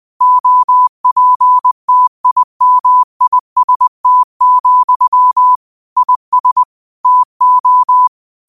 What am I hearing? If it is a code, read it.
Morse code, heard as a keyed tone, about 20 wpm, 1000 Hz. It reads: OPTIMIST, IS TO
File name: qotdfiles / 20wpm